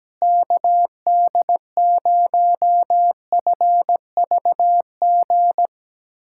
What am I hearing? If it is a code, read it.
Morse code, heard as a keyed tone, about 17 wpm, 700 Hz. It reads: KD0FVG
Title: USA Callsign DE Callsign - 6 Characters - 17wpm